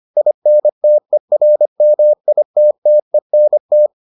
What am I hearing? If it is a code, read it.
Morse code, heard as a keyed tone, about 25 wpm, 600 Hz. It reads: INTERMITTENT